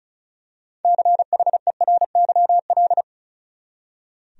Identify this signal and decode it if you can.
Morse code, heard as a keyed tone, about 34 wpm, 700 Hz. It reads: CHERYL
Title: USA Common Names - 35wpm